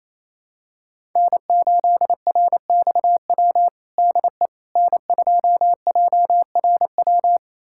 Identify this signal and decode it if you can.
Morse code, heard as a keyed tone, about 28 wpm, 700 Hz. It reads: N8RXW DE N2JRW
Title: USA Callsign DE Callsign - 5 Characters - 28wpm